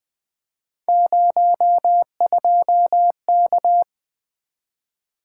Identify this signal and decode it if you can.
Morse code, heard as a keyed tone, about 23 wpm, 700 Hz. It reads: 02K
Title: Speed-Racing - Three Letter-Number Combinations - 20wpm